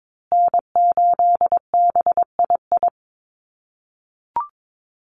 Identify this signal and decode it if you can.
Morse code, heard as a keyed tone, about 22 wpm, 700 Hz. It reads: N8BII
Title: USA Call Signs Repeated 2x 22wpm